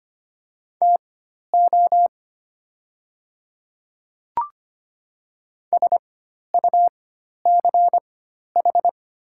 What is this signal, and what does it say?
Morse code, heard as a keyed tone, about 25 wpm, spaced as for 12 wpm, 700 Hz. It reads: TO SUCH
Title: 4x Character Spacing - Top 200 Words - 25wpm